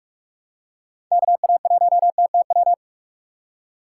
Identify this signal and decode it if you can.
Morse code, heard as a keyed tone, about 46 wpm, 700 Hz. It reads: KA1TTW
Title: USA Call Signs - 6 Characters - 6x Character Spacing - 45wpm